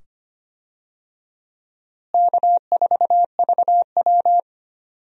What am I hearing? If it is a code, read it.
Morse code, heard as a keyed tone, about 25 wpm, 700 Hz. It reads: K4VW
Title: USA Call Signs - 4 Characters - Repeated 2x - 25wpm